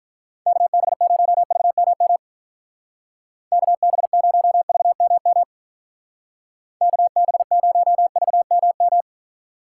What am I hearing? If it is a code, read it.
Morse code, heard as a keyed tone, about 44 wpm, 700 Hz. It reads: KB0VMM KB0VMM KB0VMM